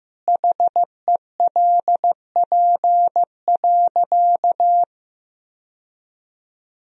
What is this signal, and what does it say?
Morse code, heard as a keyed tone, about 16 wpm, 700 Hz. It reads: HELP.
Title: Sentences from Top 1000 Words - Review Part 4 - 15wpm